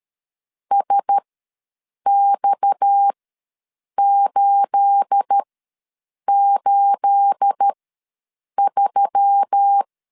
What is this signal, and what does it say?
Morse code, heard as a keyed tone, about 13 wpm, spaced as for 7 wpm, 800 Hz. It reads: SX883
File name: LEZIONI / 9 28 ws 12wpm 3354